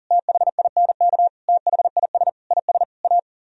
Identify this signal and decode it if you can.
Morse code, heard as a keyed tone, about 51 wpm, 700 Hz. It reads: THINK THIS IS A